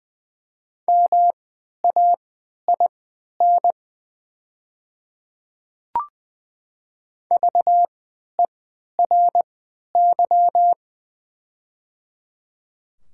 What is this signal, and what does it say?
Morse code, heard as a keyed tone, about 20 wpm, spaced as for 11 wpm, 700 Hz. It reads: MAIN VERY
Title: 3x Character Spacing - Top 300 Words - 20wpm